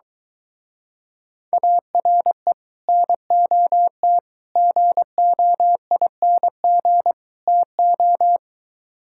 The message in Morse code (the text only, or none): ARE NOT GOING TO